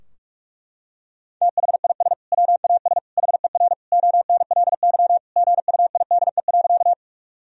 Morse code, heard as a keyed tone, about 41 wpm, 700 Hz. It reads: THIS WAS HER ONLY GUIDE.
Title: Sentences from Top 1000 Words - Review Part 3 - 45wpm